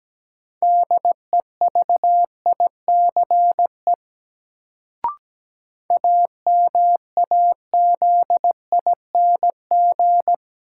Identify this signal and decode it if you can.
Morse code, heard as a keyed tone, about 17 wpm, 700 Hz. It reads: DEVICE AMAZING
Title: Sentences from Top 1500 Words - New Word Drill - 17wpm